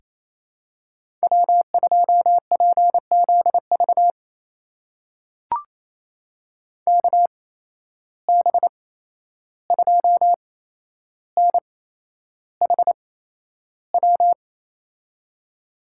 Morse code, heard as a keyed tone, about 28 wpm, spaced as for 11 wpm, 700 Hz. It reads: W2PZV KB2NHW
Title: USA Call Signs - 8x Character Spacing - 28wpm